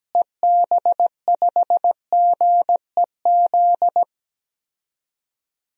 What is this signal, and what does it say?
Morse code, heard as a keyed tone, about 17 wpm, 700 Hz. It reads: EB5GEZ
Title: DXCC-100 Call Signs - 6 Characters - Repeated 2x - 17wpm